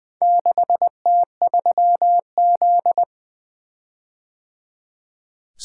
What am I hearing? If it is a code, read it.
Morse code, heard as a keyed tone, about 25 wpm, 700 Hz. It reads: 6T3Z
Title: Speed-Racing - Four Letter-Number Combinations - 20wpm